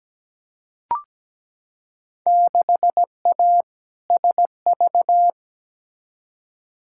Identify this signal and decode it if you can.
Morse code, heard as a keyed tone, about 17 wpm, 700 Hz. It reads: 6A SV
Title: Field Day Exchanges Encore - 17wpm